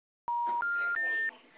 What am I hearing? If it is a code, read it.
Special information tone, with speech.